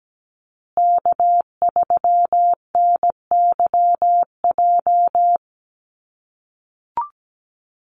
Morse code, heard as a keyed tone, about 17 wpm, 700 Hz. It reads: K3NYJ
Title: USA Call Signs Repeated 3x 17wpm